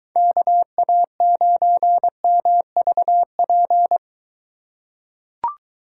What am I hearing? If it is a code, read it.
Morse code, heard as a keyed tone, about 23 wpm, 700 Hz. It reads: KA9MVP